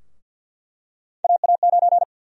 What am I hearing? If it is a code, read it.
Morse code, heard as a keyed tone, about 47 wpm, 700 Hz. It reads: AA9
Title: USA Call Sign Prefixes - 50wpm